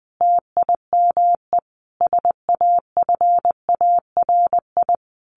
Morse code, heard as a keyed tone, about 20 wpm, 700 Hz. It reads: TIME SAFARI
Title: A Sound Of Thunder 20wpm